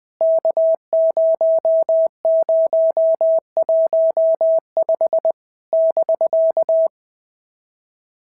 Morse code, heard as a keyed tone, about 20 wpm, 650 Hz. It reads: K0015 <BK>
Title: POTA QSO Practice - 20wpm